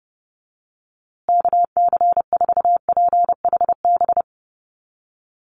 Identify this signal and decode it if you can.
Morse code, heard as a keyed tone, about 30 wpm, 700 Hz. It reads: KC4PHB
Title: USA Call Signs Repeated 2x 30wpm